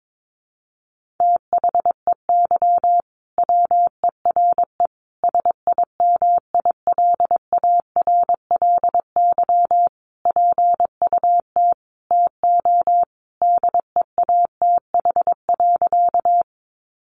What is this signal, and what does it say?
Morse code, heard as a keyed tone, about 22 wpm, 700 Hz. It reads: THEY WERE SIMILARLY PUT TO DEATH.
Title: Sentences from Top 1400 Words 22wpm